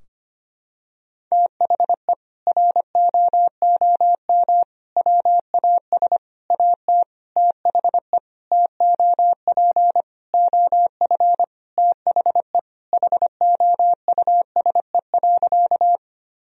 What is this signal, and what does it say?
Morse code, heard as a keyed tone, about 25 wpm, 700 Hz. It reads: THE ROOM WAS AT THE TOP OF THE HOUSE.
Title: Sentences from Top 1000 Words - Review Part 2 - 25wpm